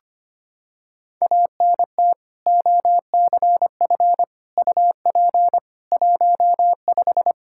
Morse code, heard as a keyed tone, about 25 wpm, 700 Hz. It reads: ANT OCF UP 15
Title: Practice QSO - Exchange 2 - Info - 25wpm